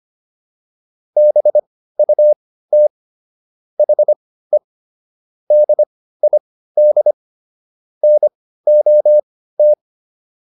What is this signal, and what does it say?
Morse code, heard as a keyed tone, about 25 wpm, spaced as for 15 wpm, 600 Hz. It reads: BUT HE DID NOT
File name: WD305 15